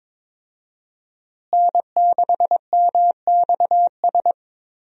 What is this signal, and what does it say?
Morse code, heard as a keyed tone, about 22 wpm, 700 Hz. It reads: N6MXS
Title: USA Call Signs - 5 Characters - Repeated 3x - 22wpm